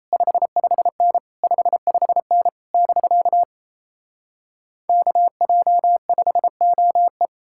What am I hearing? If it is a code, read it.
Morse code, heard as a keyed tone, about 38 wpm, 700 Hz. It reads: 55N 55N <BK> KJ5OE